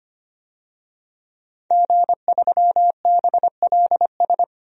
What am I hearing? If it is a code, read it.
Morse code, heard as a keyed tone, about 25 wpm, 700 Hz. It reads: G3BLS
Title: DXCC-100 Call Signs - 5 Characters - 25wpm